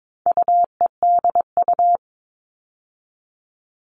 Morse code, heard as a keyed tone, about 22 wpm, 700 Hz. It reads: UEDU
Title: All Cut Numbers - 4-Digit 22wpm